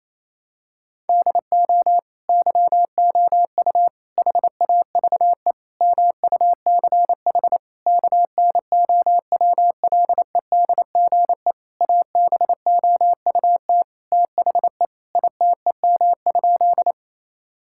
Morse code, heard as a keyed tone, about 28 wpm, 700 Hz. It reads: DO YOU HAVE MUCH KNOWLEDGE ABOUT THE ITEM?
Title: Sentences from Top 1000 Words - Review Part 6 - 28wpm